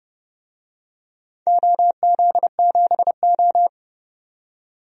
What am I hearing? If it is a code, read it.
Morse code, heard as a keyed tone, about 30 wpm, 700 Hz. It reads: OZ7O
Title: DXCC-100 Call Signs - 4 Characters - 30wpm